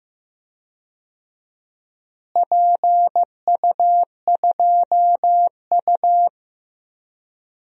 Morse code, heard as a keyed tone, about 15 wpm, 700 Hz. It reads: PU2U
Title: DXCC-100 Call Signs - 4 Characters - Repeated 3x - 15wpm